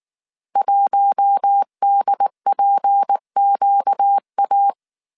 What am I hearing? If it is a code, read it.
Morse code, heard as a keyed tone, about 19 wpm, 800 Hz. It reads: 1DPQA
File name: CW / DET / 1 6 1500C S A 18wpm 2255